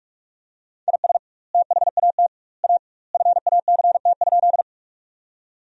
Morse code, heard as a keyed tone, about 46 wpm, 700 Hz. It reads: IS THAT A FACT?
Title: 2x Word-Spacing - Sentences from Top 500 Words - 45wpm